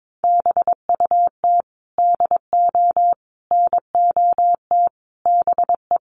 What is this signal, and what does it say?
Morse code, heard as a keyed tone, about 22 wpm, 700 Hz. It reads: BUT DO NOT BE
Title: Sentences from Top 1500 Words 22wpm